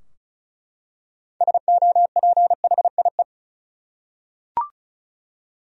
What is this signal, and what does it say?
Morse code, heard as a keyed tone, about 36 wpm, 700 Hz. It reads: SOPHIE
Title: USA Common Names - 35wpm